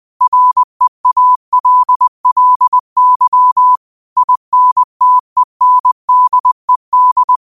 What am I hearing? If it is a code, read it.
Morse code, heard as a keyed tone, about 20 wpm, 1000 Hz. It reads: REALLY INTENDED